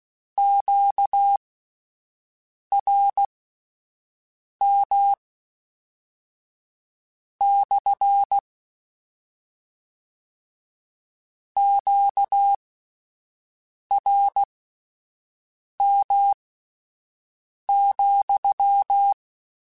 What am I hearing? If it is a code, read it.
Morse code, heard as a keyed tone, about 16 wpm, spaced as for 5 wpm, 800 Hz. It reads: QRM / QRM,
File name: Q Calls / Q Calls-mp3s / Q Calls SPSL2 @10